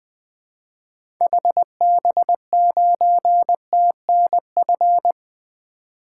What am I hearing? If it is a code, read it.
Morse code, heard as a keyed tone, about 20 wpm, 700 Hz. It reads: HB9TNF
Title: 2023 SOTA Activator Call Signs - 20wpm